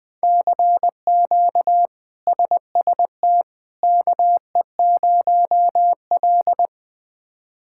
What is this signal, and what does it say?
Morse code, heard as a keyed tone, about 20 wpm, 700 Hz. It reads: CQ SST KE0L
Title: K1USN SST QSO Practice - 20wpm